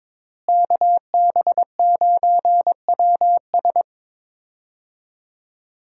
Morse code, heard as a keyed tone, about 22 wpm, 700 Hz. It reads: KB9WS
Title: USA Call Signs - 5 Characters - Repeated 2x - 22wpm